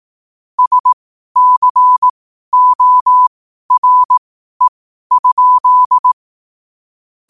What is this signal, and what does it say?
Morse code, heard as a keyed tone, about 18 wpm, spaced as for 12 wpm, 1000 Hz. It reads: SCORE?
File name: qotdfiles / 12wpm